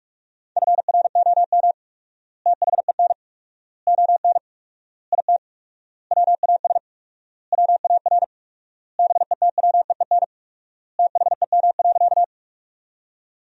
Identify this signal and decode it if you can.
Morse code, heard as a keyed tone, about 46 wpm, 700 Hz. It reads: FROM THEN ON IT WAS WAR BETWEEN THEM.
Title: 4x Word-Spacing - Sentences from Top 400 Words - 45wpm